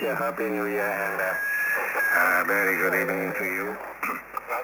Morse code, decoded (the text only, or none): none